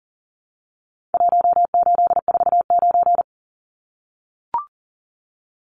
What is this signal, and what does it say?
Morse code, heard as a keyed tone, about 40 wpm, 700 Hz. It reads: 1849
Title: Numbers - Contest Style - 4-Digit 40wpm